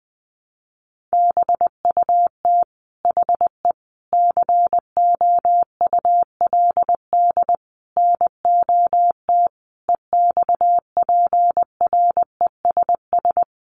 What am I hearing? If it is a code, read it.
Morse code, heard as a keyed tone, about 20 wpm, 700 Hz. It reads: BUT HE COULD NOT EXPRESS
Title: Sentences from Top 1100 Words 20wpm